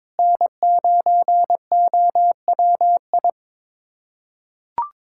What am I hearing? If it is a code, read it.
Morse code, heard as a keyed tone, about 22 wpm, 700 Hz. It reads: N9OWI
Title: USA Callsign DE Callsign 22wpm